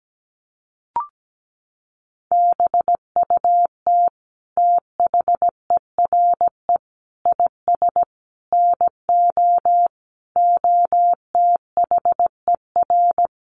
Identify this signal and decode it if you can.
Morse code, heard as a keyed tone, about 17 wpm, 700 Hz. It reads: BUT THERE IS NO OTHER
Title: Sentences from Top 1100 Words 17wpm